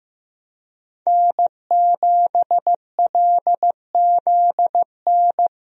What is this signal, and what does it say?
Morse code, heard as a keyed tone, about 15 wpm, 700 Hz. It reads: N7LZN
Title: USA Call Signs - 5 Characters - Repeated 2x - 15wpm